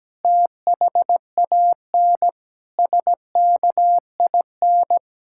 Morse code, heard as a keyed tone, about 17 wpm, 700 Hz. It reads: THAN SKIN
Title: Sentences from Top 1600 Words 17wpm